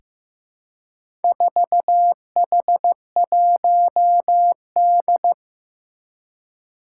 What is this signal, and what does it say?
Morse code, heard as a keyed tone, about 15 wpm, 700 Hz. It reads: 4H1D
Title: Four Challenging Letter-Number Combinations 15wpm